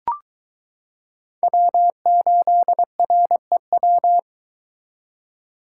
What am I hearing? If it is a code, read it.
Morse code, heard as a keyed tone, about 22 wpm, 700 Hz. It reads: W8REW